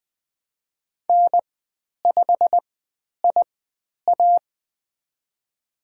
Morse code, heard as a keyed tone, about 20 wpm, spaced as for 10 wpm, 700 Hz. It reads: N5IA
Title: 102 Lesson 17g - Letter u - Call Signs 20wpm